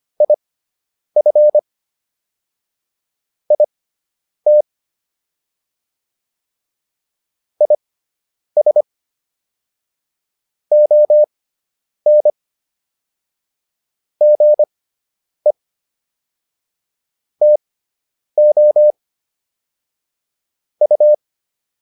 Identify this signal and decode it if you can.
Morse code, heard as a keyed tone, about 25 wpm, spaced as for 9 wpm, 600 Hz. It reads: IF IT IS ON GE TO U